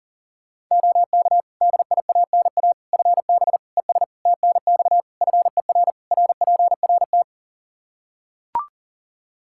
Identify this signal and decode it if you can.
Morse code, heard as a keyed tone, about 40 wpm, 700 Hz. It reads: OK DIANA FB ES TNX FER RPRT
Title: Practice QSO - Exchange 2 - Info - 40wpm